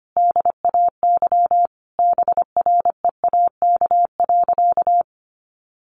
Morse code, heard as a keyed tone, about 25 wpm, 700 Hz. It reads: DAY BREAK.
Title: Sentences from Top 1300 Words 25wpm